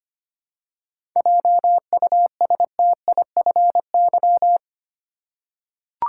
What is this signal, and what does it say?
Morse code, heard as a keyed tone, about 25 wpm, 700 Hz. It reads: JUSTIFY
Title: Seven Letter Words - 25wpm